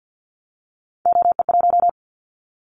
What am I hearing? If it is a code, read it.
Morse code, heard as a keyed tone, about 51 wpm, 700 Hz. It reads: OE1
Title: Callsign Prefixes for Top 100 DXCC - 50wpm